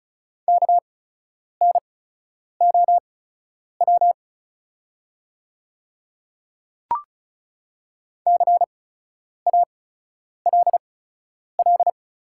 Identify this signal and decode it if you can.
Morse code, heard as a keyed tone, about 36 wpm, spaced as for 10 wpm, 700 Hz. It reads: KNOW CALL
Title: 8x Character Spacing - Top 100 Words - 35wpm